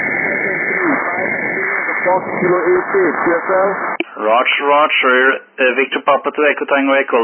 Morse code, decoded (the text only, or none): none